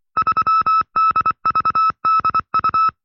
Morse code, heard as a keyed tone, about 24 wpm, 1310 Hz. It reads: 3DVDU
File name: FreeFiles / CW-practice / 20WPM-side2